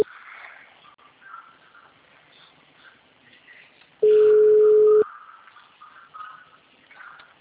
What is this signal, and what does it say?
Ringback tone (Europe, 425 Hz).